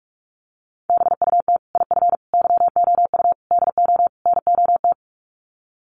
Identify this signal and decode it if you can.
Morse code, heard as a keyed tone, about 44 wpm, 700 Hz. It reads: BUT IF YOU DO NOT
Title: Sets of 5 Words Encore 45wpm